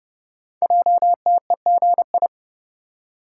Morse code, heard as a keyed tone, about 30 wpm, 700 Hz. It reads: JTEGI
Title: Code Groups - 5 Letters - 30wpm